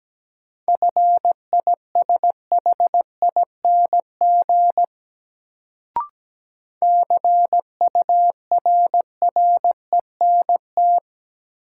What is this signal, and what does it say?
Morse code, heard as a keyed tone, about 17 wpm, 700 Hz. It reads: FISHING CURRENT
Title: Sentences from Top 1600 Words - New Word Drill Alt - 17wpm